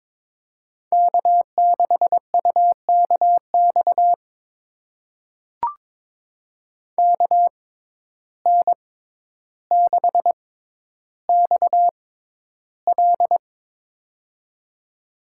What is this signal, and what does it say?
Morse code, heard as a keyed tone, about 22 wpm, spaced as for 11 wpm, 700 Hz. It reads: K6UKX KN6XL
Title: USA Call Signs - 5 Characters - 6x Character Spacing - 22wpm